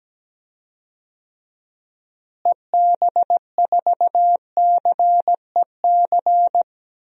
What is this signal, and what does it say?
Morse code, heard as a keyed tone, about 17 wpm, 700 Hz. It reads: EB4CEC
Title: DXCC-100 Call Signs - 6 Characters - Repeated 3x - 17wpm